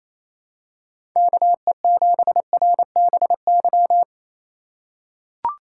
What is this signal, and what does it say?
Morse code, heard as a keyed tone, about 28 wpm, 700 Hz. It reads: KE7RBY
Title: USA Call Signs - 6 Characters - 2x Character Spacing - 28wpm